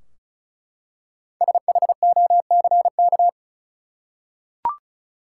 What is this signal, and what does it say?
Morse code, heard as a keyed tone, about 36 wpm, 700 Hz. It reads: SHOCK
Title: Five Letter Words - 35wpm